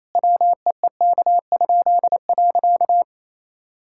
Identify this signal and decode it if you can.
Morse code, heard as a keyed tone, about 27 wpm, 700 Hz. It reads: WEEK?.
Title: Pangram Challenge v2 - 28wpm